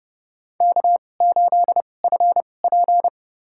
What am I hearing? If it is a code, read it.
Morse code, heard as a keyed tone, about 30 wpm, spaced as for 22 wpm, 700 Hz. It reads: K8FP
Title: USA Call Signs - 2x Character Spacing - 30wpm